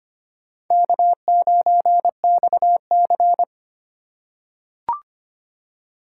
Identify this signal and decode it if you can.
Morse code, heard as a keyed tone, about 25 wpm, 700 Hz. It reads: K9XC
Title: USA Call Signs - 4 Characters - 3x Character Spacing - 25wpm